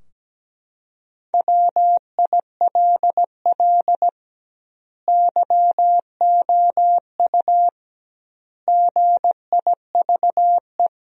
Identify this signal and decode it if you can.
Morse code, heard as a keyed tone, about 17 wpm, 700 Hz. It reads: WILL YOU GIVE
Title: 2x Word-Spacing - Sentences from Top 200 Words - 17wpm